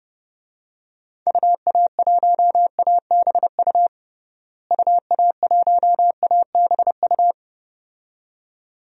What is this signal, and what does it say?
Morse code, heard as a keyed tone, about 32 wpm, 700 Hz. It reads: UA1ABU UA1ABU
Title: DXCC-100 Call Signs - 6 Characters - Repeated 2x - 30wpm